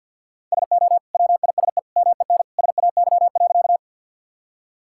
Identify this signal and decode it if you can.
Morse code, heard as a keyed tone, about 50 wpm, 700 Hz. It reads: SO WISE MEN SAY.
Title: Sentences from Top 1600 Words 50wpm